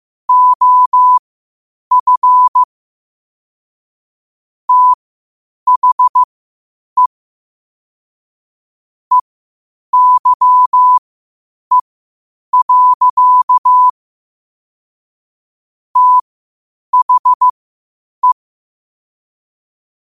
Morse code, heard as a keyed tone, about 15 wpm, spaced as for 8 wpm, 1000 Hz. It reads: OF THE EYE. THE
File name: qotdfiles / 8wpm